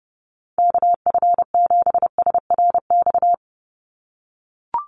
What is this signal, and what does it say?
Morse code, heard as a keyed tone, about 30 wpm, 700 Hz. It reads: KF7SRX
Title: USA Call Signs 30wpm